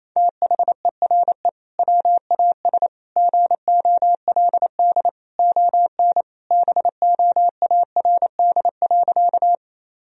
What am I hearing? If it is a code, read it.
Morse code, heard as a keyed tone, about 28 wpm, 700 Hz. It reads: THERE WAS GOLD ON BOARD.
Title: Sentences from Top 1000 Words - Review Part 4 - 28wpm